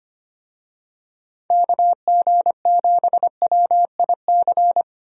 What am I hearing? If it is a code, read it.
Morse code, heard as a keyed tone, about 25 wpm, 700 Hz. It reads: KG7WIC
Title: USA Call Signs - 6 Characters - Repeated 2x - 25wpm